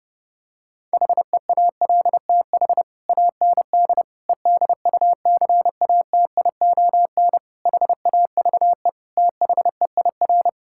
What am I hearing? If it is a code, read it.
Morse code, heard as a keyed tone, about 30 wpm, 700 Hz. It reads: HEALTH AND EDUCATION HAVE THEIR
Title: Sentences from Top 1000 Words - Review Part 6 - 30wpm